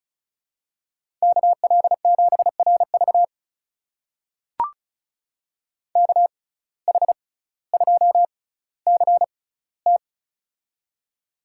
Morse code, heard as a keyed tone, about 36 wpm, spaced as for 18 wpm, 700 Hz. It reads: KL7RV KH2CT
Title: USA Call Signs - 5 Characters - 6x Character Spacing - 35wpm